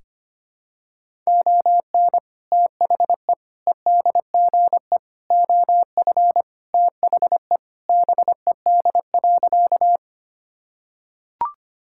Morse code, heard as a keyed tone, about 25 wpm, 700 Hz. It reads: ON THE EDGE OF THE BED.
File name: practice / 25wpm / Sentences with 3-Word Phrases - 25wpm